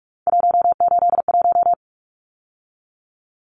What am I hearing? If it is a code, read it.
Morse code, heard as a keyed tone, about 44 wpm, 700 Hz. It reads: 181
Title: Numbers - 3-Digit 45wpm